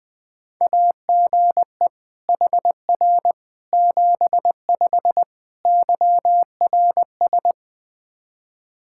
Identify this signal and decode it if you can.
Morse code, heard as a keyed tone, about 21 wpm, 700 Hz. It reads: AGE HR 75 YRS
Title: K1USN SST QSO Practice 20wpm